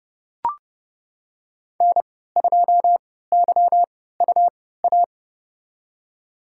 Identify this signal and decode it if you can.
Morse code, heard as a keyed tone, about 30 wpm, spaced as for 17 wpm, 700 Hz. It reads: N2YUA
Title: USA Call Signs - 5 Characters - 3x Character Spacing - 30wpm